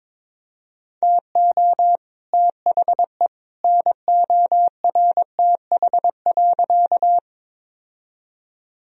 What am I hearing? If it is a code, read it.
Morse code, heard as a keyed tone, about 22 wpm, 700 Hz. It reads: TO THE NORTH.
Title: Sentences from Top 1000 Words - Review Part 1 - 22wpm